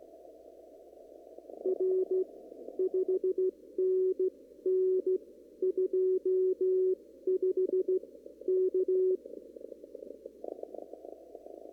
Morse code, heard as a keyed tone, about 15 wpm, spaced as for 13 wpm, 370 Hz. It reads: R 5NN 25 K
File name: CW / kt5j wpx cw-op-k5pi